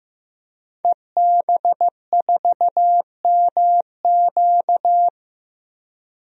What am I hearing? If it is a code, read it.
Morse code, heard as a keyed tone, about 15 wpm, 700 Hz. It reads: EB4MQ
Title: DXCC-100 Call Signs - 5 Characters - Repeated 3x - 15wpm